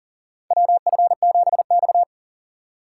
Morse code, heard as a keyed tone, about 40 wpm, 700 Hz. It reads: WF7X
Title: USA Call Signs - 3x Character Spacing - 40wpm